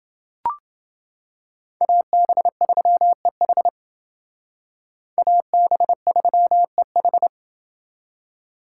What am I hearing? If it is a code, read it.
Morse code, heard as a keyed tone, about 28 wpm, 700 Hz. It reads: AB3EH AB3EH